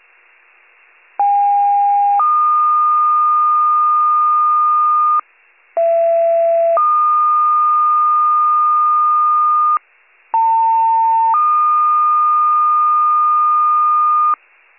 Two-tone page, heard as a one-second tone, then a three-second tone, about 810 then 1220 Hz, about 670 then 1220 Hz, about 900 then 1220 Hz. It is heard from 1.2 to 14.4 s.